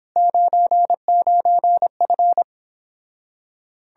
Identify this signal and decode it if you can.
Morse code, heard as a keyed tone, about 29 wpm, 700 Hz. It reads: 99F